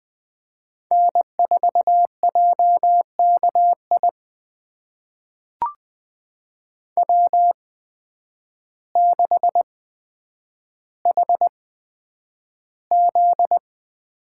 Morse code, heard as a keyed tone, about 20 wpm, spaced as for 9 wpm, 700 Hz. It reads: N4JKI W6HZ
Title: USA Call Signs - 5 Characters - 8x Character Spacing - 20wpm